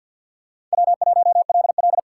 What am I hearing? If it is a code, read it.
Morse code, heard as a keyed tone, about 48 wpm, 700 Hz. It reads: W1LL
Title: USA Call Signs - 4 Characters - 6x Character Spacing - 50wpm